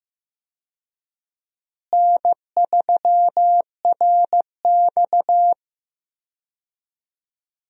Morse code, heard as a keyed tone, about 15 wpm, 700 Hz. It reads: N3RX
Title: USA Call Signs - 4 Characters - Repeated 3x - 15wpm